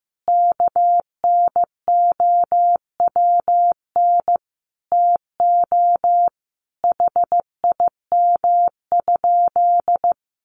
Morse code, heard as a keyed tone, about 14 wpm, 700 Hz. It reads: KNOWN TO HIM?
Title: Sentences from Top 700 Words 15wpm